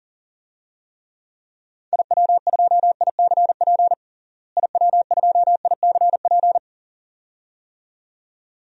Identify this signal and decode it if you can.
Morse code, heard as a keyed tone, about 40 wpm, 700 Hz. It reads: IW2ICP IW2ICP
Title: DXCC-100 Call Signs - 6 Characters - Repeated 2x - 40wpm